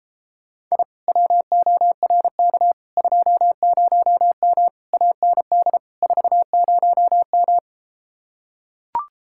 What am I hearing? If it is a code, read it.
Morse code, heard as a keyed tone, about 34 wpm, 700 Hz. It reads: I WORK 20M AND 40M